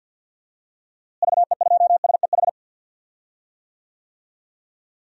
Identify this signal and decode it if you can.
Morse code, heard as a keyed tone, about 46 wpm, 700 Hz. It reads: VE2SEH
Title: DXCC-100 Call Signs - 6 Characters - Repeated 2x - 50wpm